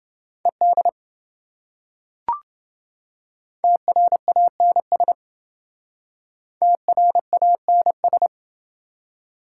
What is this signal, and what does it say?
Morse code, heard as a keyed tone, about 28 wpm, 700 Hz. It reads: ED TRANS TRANS